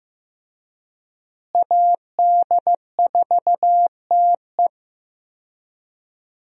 Morse code, heard as a keyed tone, about 15 wpm, 700 Hz. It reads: AD4TE
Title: USA Call Signs - 5 Characters - Repeated 2x - 15wpm